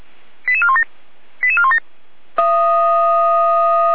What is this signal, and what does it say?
Five-tone selective call (ZVEI), heard as quick runs of five tones, tones of about 70 ms. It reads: 88417 88417